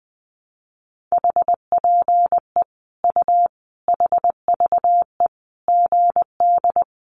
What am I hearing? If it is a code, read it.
Morse code, heard as a keyed tone, about 19 wpm, 700 Hz. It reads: HPE U HVE GD
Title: Common QSO Phrases 20wpm